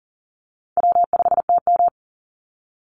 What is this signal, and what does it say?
Morse code, heard as a keyed tone, about 40 wpm, 700 Hz. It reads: W5TM
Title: Easy Call Signs Repeated 2x - 40wpm